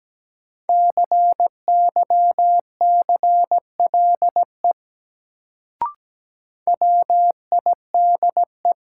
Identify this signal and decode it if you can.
Morse code, heard as a keyed tone, about 17 wpm, 700 Hz. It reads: CYCLE WIDE
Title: Sentences from Top 900 Words - New Word Drill - 17wpm